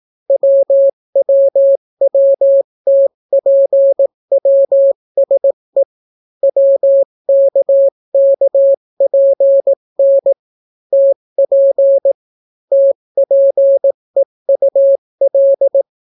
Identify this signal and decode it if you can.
Morse code, heard as a keyed tone, about 18 wpm, 550 Hz. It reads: WWWTPWSE WKKPN TP TPEUL